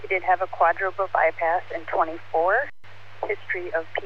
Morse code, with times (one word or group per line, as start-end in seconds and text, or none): none